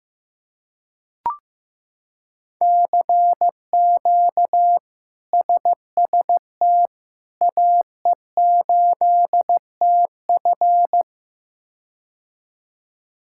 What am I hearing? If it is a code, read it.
Morse code, heard as a keyed tone, about 15 wpm, 700 Hz. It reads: CQ SST AE8TF
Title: K1USN SST QSO Practice - 15wpm